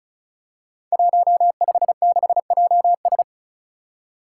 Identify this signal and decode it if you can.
Morse code, heard as a keyed tone, about 36 wpm, 700 Hz. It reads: 156JS
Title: Five Challenging Letter-Number Combinations 35wpm